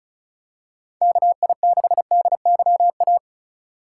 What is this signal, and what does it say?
Morse code, heard as a keyed tone, about 34 wpm, 700 Hz. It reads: KI6DYA
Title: USA Call Signs - 6 Characters - 3x Character Spacing - 35wpm